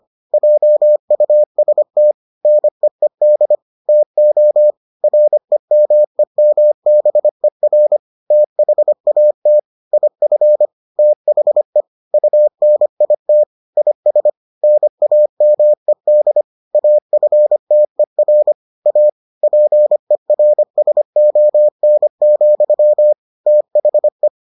Morse code, heard as a keyed tone, about 25 wpm, 600 Hz. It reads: JUST NEED TO REMEMBER THAT IF THE UNIT IS NAMED AFTER A PERSON, THE